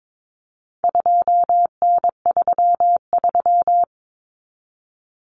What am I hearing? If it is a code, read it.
Morse code, heard as a keyed tone, about 22 wpm, 700 Hz. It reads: 2N33
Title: Common Cut Numbers - 4-Digit 22wpm